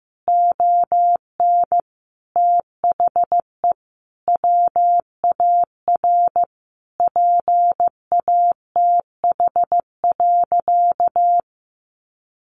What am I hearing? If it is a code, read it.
Morse code, heard as a keyed tone, about 14 wpm, 700 Hz. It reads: ON THE WAR PATH.
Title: Sentences from Top 1400 Words 15wpm